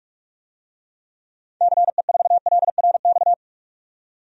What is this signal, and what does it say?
Morse code, heard as a keyed tone, about 46 wpm, 700 Hz. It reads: CE4LRX
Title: DXCC-100 Call Signs - 6 Characters - 45wpm